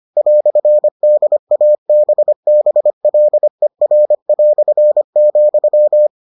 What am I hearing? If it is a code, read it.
Morse code, heard as a keyed tone, about 25 wpm, 600 Hz. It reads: "DABBLER",